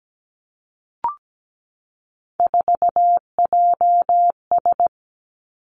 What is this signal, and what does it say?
Morse code, heard as a keyed tone, about 17 wpm, 700 Hz. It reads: 4JS
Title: Three Challenging Letter-Number Combinations 17wpm